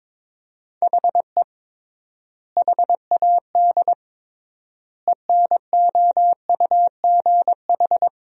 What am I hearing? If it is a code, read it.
Morse code, heard as a keyed tone, about 22 wpm, 700 Hz. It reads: HE HAD ENOUGH
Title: 3x Word-Spacing - Sentences from Top 400 Words - 22wpm